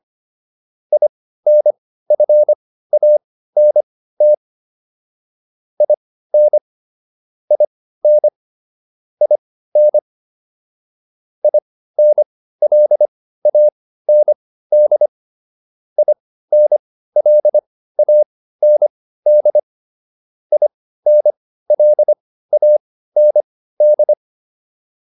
Morse code, heard as a keyed tone, about 25 wpm, spaced as for 15 wpm, 600 Hz. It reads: INFANT IN IN IN INLAND INLAND INLAND